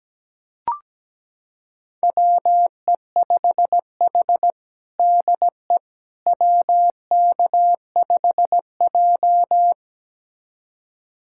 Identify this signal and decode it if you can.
Morse code, heard as a keyed tone, about 17 wpm, 700 Hz. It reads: WE5H DE WK5J
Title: USA Callsign DE Callsign - 4 Characters - 17wpm